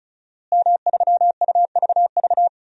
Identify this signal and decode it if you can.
Morse code, heard as a keyed tone, about 36 wpm, 700 Hz. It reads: M3UVV
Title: DXCC-100 Call Signs - 5 Characters - Repeated 3x - 35wpm